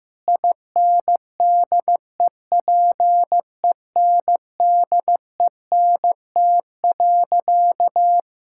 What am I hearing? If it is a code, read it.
Morse code, heard as a keyed tone, about 15 wpm, 700 Hz. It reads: INDEPENDENT.
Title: Sentences from Top 1000 Words - Review Part 6 - 15wpm